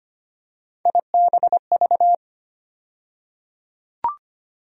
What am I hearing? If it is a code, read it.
Morse code, heard as a keyed tone, about 25 wpm, 700 Hz. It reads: IBV
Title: USA Call Sign Postfixes - 3 Letters 25wpm